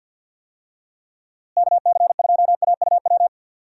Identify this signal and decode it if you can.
Morse code, heard as a keyed tone, about 48 wpm, 700 Hz. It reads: KC2AUW
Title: USA Call Signs - 6 Characters - Repeated 2x - 50wpm